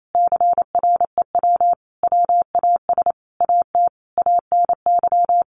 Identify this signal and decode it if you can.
Morse code, heard as a keyed tone, about 28 wpm, 700 Hz. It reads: CREW WAS AT ANY
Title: Sentences from Top 1500 Words 28wpm